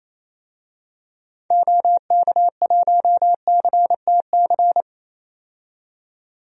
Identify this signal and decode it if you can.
Morse code, heard as a keyed tone, about 28 wpm, 700 Hz. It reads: OK1CTC
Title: DXCC-100 Call Signs - 6 Characters - 28wpm